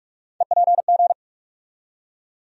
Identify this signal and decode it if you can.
Morse code, heard as a keyed tone, about 45 wpm, 700 Hz. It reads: EPG
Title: USA Call Sign Postfixes - 3 Letters 45wpm